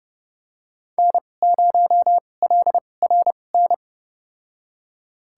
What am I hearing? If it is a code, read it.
Morse code, heard as a keyed tone, about 30 wpm, spaced as for 22 wpm, 700 Hz. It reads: N0LRN
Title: USA Call Signs - 5 Characters - 2x Character Spacing - 30wpm